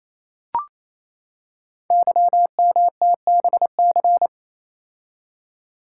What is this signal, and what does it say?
Morse code, heard as a keyed tone, about 27 wpm, 700 Hz. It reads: YMTBC
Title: Code Groups - 5 Letters - 28wpm